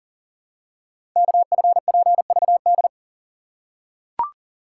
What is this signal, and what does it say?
Morse code, heard as a keyed tone, about 42 wpm, 690 Hz. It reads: KFPVD
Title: Code Groups - 5 Letters - 40wpm